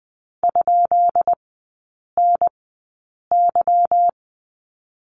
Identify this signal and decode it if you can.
Morse code, heard as a keyed tone, about 20 wpm, spaced as for 8 wpm, 690 Hz. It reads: ?NY
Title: 145 Lesson 24e - Letter y - Three Characters 20wpm